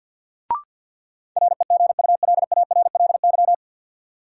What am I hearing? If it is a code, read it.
Morse code, heard as a keyed tone, about 50 wpm, 700 Hz. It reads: REGULARLY
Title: Sentences from Top 1400 Words - New Word Drill Alt - 50wpm